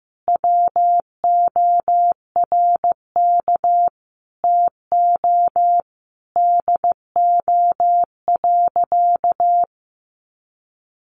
Morse code, heard as a keyed tone, about 15 wpm, 700 Hz. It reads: WORK TO DO.
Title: Sentences from Top 200 Words 15wpm